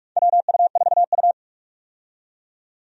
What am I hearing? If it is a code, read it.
Morse code, heard as a keyed tone, about 44 wpm, 700 Hz. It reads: WU4U
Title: USA Call Signs - 4 Characters - 45wpm